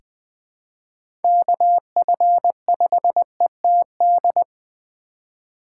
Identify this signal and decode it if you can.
Morse code, heard as a keyed tone, about 20 wpm, 700 Hz. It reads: KF5ETD
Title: K1USN SST QSO Practice - 20wpm